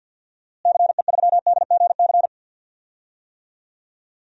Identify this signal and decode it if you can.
Morse code, heard as a keyed tone, about 50 wpm, 700 Hz. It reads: CE3DGC
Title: DXCC-100 Call Signs - 6 Characters - Repeated 2x - 50wpm